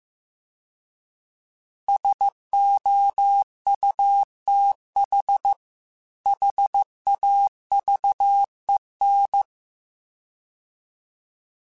Morse code, heard as a keyed tone, about 15 wpm, 800 Hz. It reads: SOUTH HAVEN